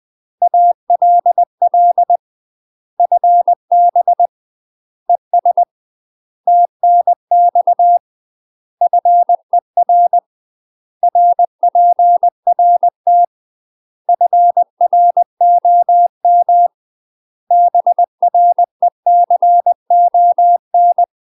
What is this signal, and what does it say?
Morse code, heard as a keyed tone, about 20 wpm, 700 Hz. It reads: ALL FB ES TNX FER RPRT FROM BRECON
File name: UK MASTER QSOs TRAINING / 20WPM / UK MASTER QSOs 0051